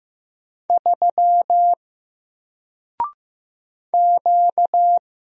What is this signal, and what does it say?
Morse code, heard as a keyed tone, about 15 wpm, 700 Hz. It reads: 3 Q
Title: Single Letter-Number Alt 15wpm